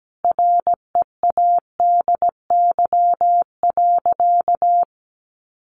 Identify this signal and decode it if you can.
Morse code, heard as a keyed tone, about 17 wpm, 700 Hz. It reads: READY.
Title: Sentences from Top 700 Words 17wpm